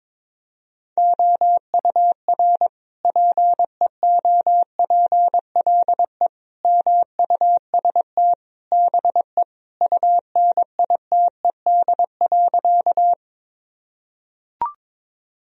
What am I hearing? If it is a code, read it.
Morse code, heard as a keyed tone, about 22 wpm, 700 Hz. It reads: OUR PEOPLE MUST BE UNITED.
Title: Sentences from Top 1000 Words - Review Part 4 - 22wpm